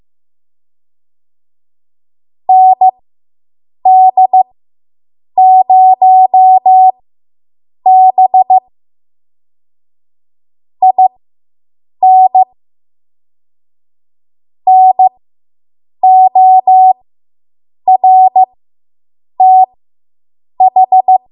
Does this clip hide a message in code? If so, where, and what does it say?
Yes: Morse code from 2.5 to 21.2 s, heard as a keyed tone, about 15 wpm, spaced as for 7 wpm, 750 Hz. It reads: ND0B IN NORTH